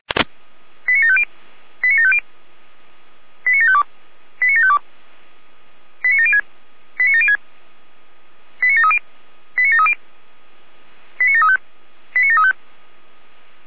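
Five-tone selective call (ZVEI), heard as quick runs of five tones, tones of about 70 ms. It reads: 89750 89750 89752 89752 89706 89706 89730 89730 89735 89735
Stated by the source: ZVEI five-tone selective call 89519, received by SDR.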